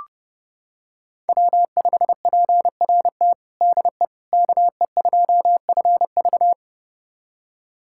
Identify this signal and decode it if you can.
Morse code, heard as a keyed tone, about 30 wpm, 700 Hz. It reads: W5PRT DE KE2FV
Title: USA Callsign DE Callsign - 5 Characters - 30wpm